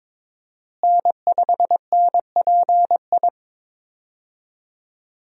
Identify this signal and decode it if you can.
Morse code, heard as a keyed tone, about 22 wpm, 700 Hz. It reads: N5NPI
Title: USA Call Signs - 5 Characters - Repeated 2x - 22wpm